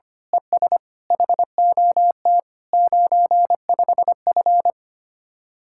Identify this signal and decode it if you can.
Morse code, heard as a keyed tone, about 25 wpm, 700 Hz. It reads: ES HOT 95F
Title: Speed-Racing Common QSO Phrases Encore - 25wpm